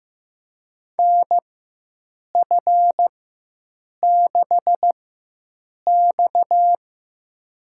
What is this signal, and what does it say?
Morse code, heard as a keyed tone, about 15 wpm, spaced as for 7 wpm, 700 Hz. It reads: NF6X
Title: USA Call Signs - 4 Characters - 4x Character Spacing - 15wpm